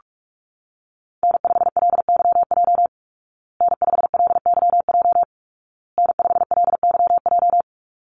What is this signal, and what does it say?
Morse code, heard as a keyed tone, about 44 wpm, 700 Hz. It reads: N5LYJ N5LYJ N5LYJ
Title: Call Signs Repeated 2x - 45wpm